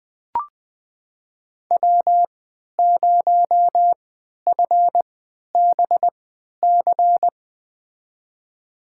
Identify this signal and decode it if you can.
Morse code, heard as a keyed tone, about 20 wpm, spaced as for 11 wpm, 700 Hz. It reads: W0FBC
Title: USA Call Signs - 5 Characters - 3x Character Spacing - 20wpm